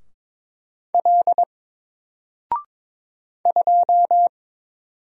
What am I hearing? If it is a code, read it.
Morse code, heard as a keyed tone, about 22 wpm, 700 Hz. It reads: L 2
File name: Single-Letter-Number-22wpm